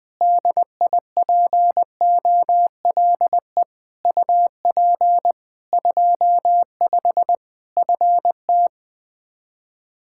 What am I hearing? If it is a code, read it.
Morse code, heard as a keyed tone, about 20 wpm, 700 Hz. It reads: DIPOLE UP 25 FT
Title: K1USN SST QSO Practice 20wpm